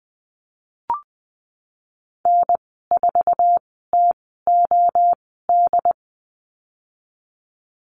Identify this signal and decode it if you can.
Morse code, heard as a keyed tone, about 20 wpm, spaced as for 15 wpm, 700 Hz. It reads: N4TOD
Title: 076 Lesson 13g - Letter L - Call Signs 20wpm